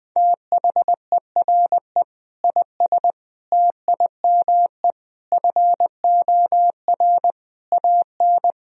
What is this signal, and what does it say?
Morse code, heard as a keyed tone, about 20 wpm, 700 Hz. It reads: THERE IS TIME FOR AN
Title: Sentences from Top 1000 Words - Review Part 3 - 20wpm